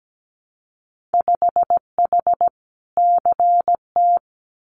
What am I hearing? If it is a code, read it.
Morse code, heard as a keyed tone, about 17 wpm, 700 Hz. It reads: 5H CT
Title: Winter Field Day Exchanges Encore 17wpm